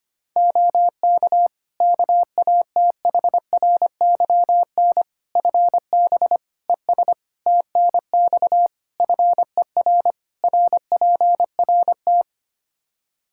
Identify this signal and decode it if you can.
Morse code, heard as a keyed tone, about 25 wpm, 700 Hz. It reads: OK KATHRYN FB ES TNX FER RPRT
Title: Practice QSO - Exchange 2 - Info - 25wpm